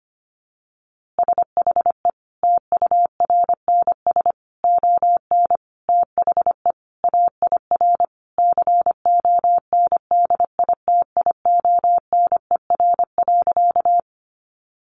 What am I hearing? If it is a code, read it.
Morse code, heard as a keyed tone, about 25 wpm, 700 Hz. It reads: SHE TURNS ON THE AIR CONDITIONER.
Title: Easy Phrases 25wpm